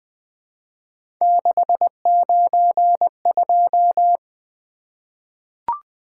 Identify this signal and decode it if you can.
Morse code, heard as a keyed tone, about 20 wpm, 700 Hz. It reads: 692
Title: Three Letter-Number Combinations - 20wpm